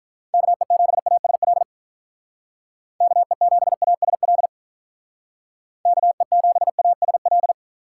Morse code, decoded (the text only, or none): KE7ASL KE7ASL KE7ASL